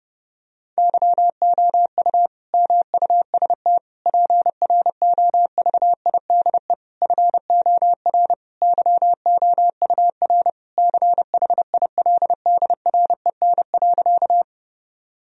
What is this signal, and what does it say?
Morse code, heard as a keyed tone, about 30 wpm, 700 Hz. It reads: YOU MUST PROVIDE FOR YOUR CHILDREN.
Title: Sentences from Top 1000 Words - Review Part 3 - 30wpm